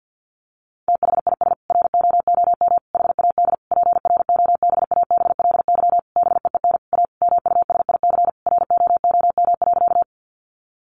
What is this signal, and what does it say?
Morse code, heard as a keyed tone, about 50 wpm, 700 Hz. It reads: THIS ROOM HAD PROBABLY BEEN A MUSIC ROOM.
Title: Sentences from Top 700 Words 50wpm